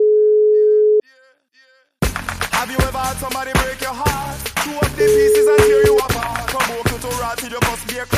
Ringback tone (Europe, 425 Hz), with noise.